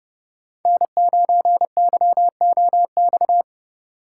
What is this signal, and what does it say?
Morse code, heard as a keyed tone, about 30 wpm, 700 Hz. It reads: N9YOX
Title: USA Call Signs - 5 Characters - 2x Character Spacing - 30wpm